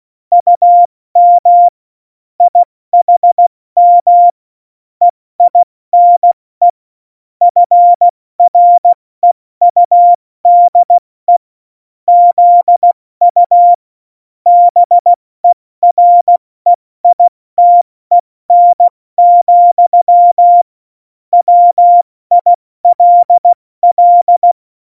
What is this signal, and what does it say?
Morse code, heard as a keyed tone, about 16 wpm, 700 Hz. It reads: UM IHM EINE FREUDE ZU BEREITEN, WILL
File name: Morsekurs / Witze / Witz 20. Oktober 2020, 70 F80